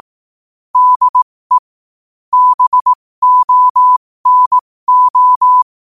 Morse code, heard as a keyed tone, about 18 wpm, spaced as for 15 wpm, 1000 Hz. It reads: DE BONO
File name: qotdfiles / 15wpm